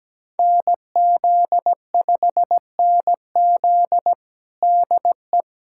Morse code, heard as a keyed tone, about 17 wpm, 700 Hz. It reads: NZ5NZ DE
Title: USA Callsign DE Callsign - 5 Characters - 17wpm